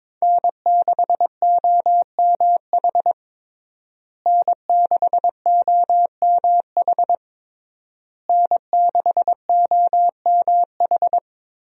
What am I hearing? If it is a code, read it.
Morse code, heard as a keyed tone, about 22 wpm, 700 Hz. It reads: N6OMH N6OMH N6OMH
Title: USA Call Signs - 5 Characters - Repeated 3x - 22wpm